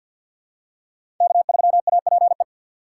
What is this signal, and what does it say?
Morse code, heard as a keyed tone, about 53 wpm, 700 Hz. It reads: K3APE
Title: USA Call Signs - 5 Characters - 2x Character Spacing - 50wpm